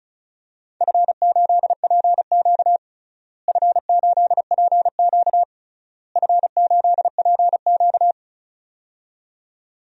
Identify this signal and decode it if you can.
Morse code, heard as a keyed tone, about 34 wpm, 700 Hz. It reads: F8PQ F8PQ F8PQ
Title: DXCC-100 Call Signs - 4 Characters - Repeated 3x - 35wpm